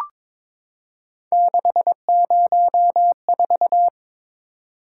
Morse code, heard as a keyed tone, about 22 wpm, 700 Hz. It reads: 604
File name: practice / 22wpm / Three Letter-Number Combinations - 22wpm